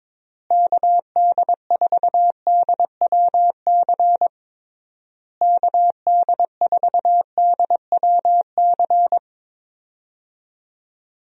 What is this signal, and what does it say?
Morse code, heard as a keyed tone, about 22 wpm, 700 Hz. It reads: KD4DWC KD4DWC
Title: USA Call Signs - 6 Characters - Repeated 3x - 22wpm